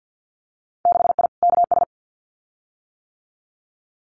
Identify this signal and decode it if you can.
Morse code, heard as a keyed tone, about 50 wpm, 700 Hz. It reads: 6I KS
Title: Winter Field Day Exchanges Encore 50wpm